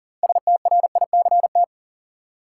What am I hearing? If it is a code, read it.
Morse code, heard as a keyed tone, about 40 wpm, 700 Hz. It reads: STRICT
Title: Six Letter Words - 40wpm